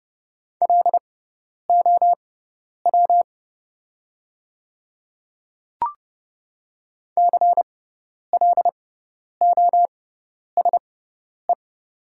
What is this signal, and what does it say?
Morse code, heard as a keyed tone, about 29 wpm, spaced as for 10 wpm, 700 Hz. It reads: LOW CLOSE
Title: 6x Character Spacing - Top 300 Words - 30wpm